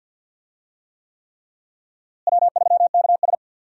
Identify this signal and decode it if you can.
Morse code, heard as a keyed tone, about 47 wpm, 700 Hz. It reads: W3KS
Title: USA Call Signs - 4 Characters - Repeated 3x - 50wpm